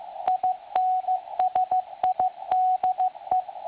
Morse code, heard as a keyed tone, about 15 wpm, 740 Hz. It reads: INSIDE
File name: attachments / websdr recording 7047.5kHz